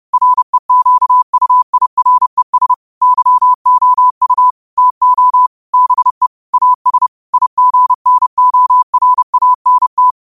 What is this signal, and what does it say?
Morse code, heard as a keyed tone, about 30 wpm, 1000 Hz. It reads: REQUIRES YOU TO BE AS IGNORANT
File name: qotdfiles / 30wpm